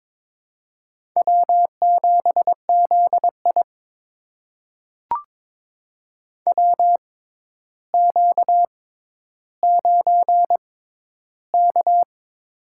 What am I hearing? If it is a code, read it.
Morse code, heard as a keyed tone, about 22 wpm, spaced as for 11 wpm, 700 Hz. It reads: W7ZI WQ9K
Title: USA Call Signs - 4 Characters - 6x Character Spacing - 22wpm